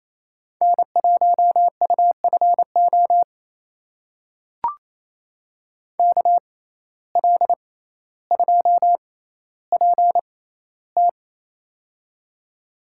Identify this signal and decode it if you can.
Morse code, heard as a keyed tone, about 28 wpm, spaced as for 14 wpm, 700 Hz. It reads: N1UFO KL2PT
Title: USA Call Signs - 5 Characters - 6x Character Spacing - 28wpm